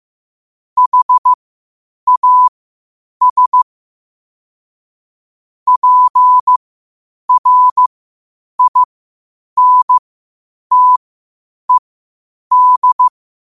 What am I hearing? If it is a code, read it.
Morse code, heard as a keyed tone, about 15 wpm, spaced as for 8 wpm, 1000 Hz. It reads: HAS PRINTED
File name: qotdfiles / 8wpm